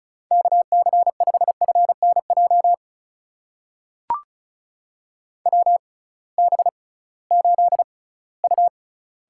Morse code, heard as a keyed tone, about 36 wpm, spaced as for 20 wpm, 700 Hz. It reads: KC5FNJ WB8U
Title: USA Call Signs - 6 Characters - 6x Character Spacing - 35wpm